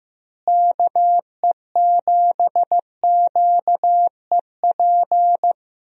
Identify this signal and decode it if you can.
Morse code, heard as a keyed tone, about 15 wpm, 700 Hz. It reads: KE7QEP
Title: USA Call Signs - 6 Characters - Repeated 3x - 15wpm